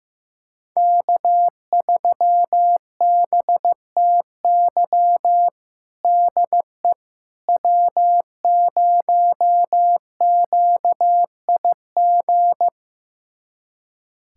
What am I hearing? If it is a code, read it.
Morse code, heard as a keyed tone, about 15 wpm, 700 Hz. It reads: K3BTY DE W0QIG
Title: USA Callsign DE Callsign - 5 Characters - 15wpm